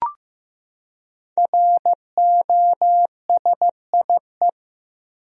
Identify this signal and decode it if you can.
Morse code, heard as a keyed tone, about 16 wpm, 700 Hz. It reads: ROSIE
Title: USA Common Names - Max 5 Letters - 15wpm